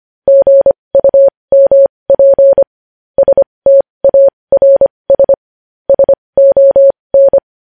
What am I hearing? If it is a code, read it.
Morse code, heard as a keyed tone, about 25 wpm, spaced as for 20 wpm, 550 Hz. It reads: GUMP STARS SON
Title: News Headlines at 20